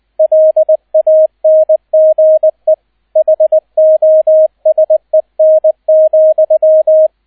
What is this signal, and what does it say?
Morse code, heard as a keyed tone, about 20 wpm, 630 Hz, with radio noise. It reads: LANGE HOSEN,